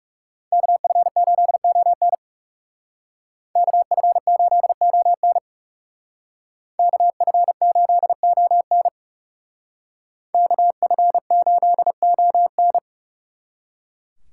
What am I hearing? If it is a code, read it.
Morse code, heard as a keyed tone, about 38 wpm, 700 Hz. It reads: KF8ON KF8ON KF8ON KF8ON